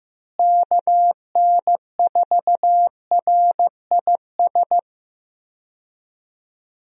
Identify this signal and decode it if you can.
Morse code, heard as a keyed tone, about 15 wpm, 700 Hz. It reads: KN4RIS
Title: USA Call Signs - 6 Characters - Repeated 3x - 15wpm